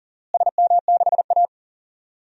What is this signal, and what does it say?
Morse code, heard as a keyed tone, about 40 wpm, 700 Hz. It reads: SM6A
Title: DXCC-100 Call Signs - 4 Characters - Repeated 2x - 40wpm